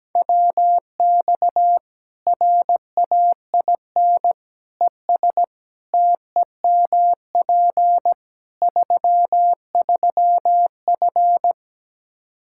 Morse code, heard as a keyed tone, about 17 wpm, 700 Hz. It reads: WX RAIN ES TEMP 33F
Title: Practice QSO - Exchange 2 - Info - 17wpm